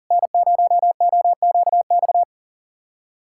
Morse code, heard as a keyed tone, about 40 wpm, 700 Hz. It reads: N0OQX
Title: USA Callsign DE Callsign - 5 Characters - 40wpm